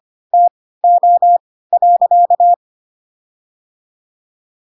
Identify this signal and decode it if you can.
Morse code, heard as a keyed tone, about 25 wpm, spaced as for 16 wpm, 700 Hz. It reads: TO.
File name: Pinocchio / Pinocchio ch6 10 25 ik1hge